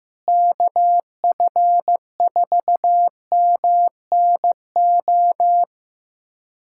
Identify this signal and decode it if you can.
Morse code, heard as a keyed tone, about 15 wpm, 700 Hz. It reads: KF4MNO
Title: USA Callsign DE Callsign - 6 Characters - 15wpm